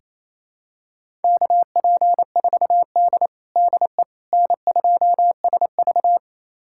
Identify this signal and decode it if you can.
Morse code, heard as a keyed tone, about 28 wpm, 700 Hz. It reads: KP4D DE N2SV
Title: USA Callsign DE Callsign - 4 Characters - 28wpm